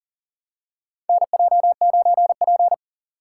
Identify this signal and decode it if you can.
Morse code, heard as a keyed tone, about 40 wpm, 700 Hz. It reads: NJ9P
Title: USA Call Signs - 4 Characters - 2x Character Spacing - 40wpm